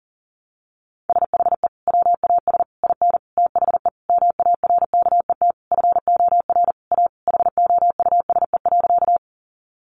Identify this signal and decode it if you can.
Morse code, heard as a keyed tone, about 40 wpm, 700 Hz. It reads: SHE WAS IN THE MARKET FOR A HOUSE.
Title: Sentences from Top 600 Words 40wpm